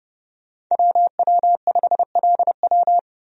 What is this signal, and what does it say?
Morse code, heard as a keyed tone, about 30 wpm, 700 Hz. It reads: WW5LW
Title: USA Call Signs - 5 Characters - 30wpm